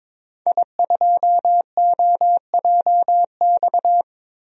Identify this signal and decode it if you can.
Morse code, heard as a keyed tone, about 22 wpm, 700 Hz. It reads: I2OJX
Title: DXCC-100 Call Signs - 5 Characters - Repeated 3x - 22wpm